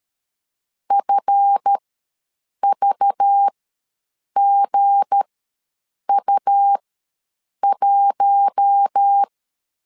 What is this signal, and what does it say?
Morse code, heard as a keyed tone, about 13 wpm, spaced as for 7 wpm, 800 Hz. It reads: FVGU1
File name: LEZIONI / 7 GF9Z / 7 GF ws 12wpm 3140